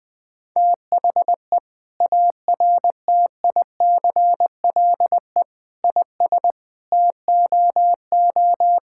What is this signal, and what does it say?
Morse code, heard as a keyed tone, about 20 wpm, 700 Hz. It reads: THE ARTICLE IS TOO
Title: Sentences from Top 1000 Words - Review Part 5 - 20wpm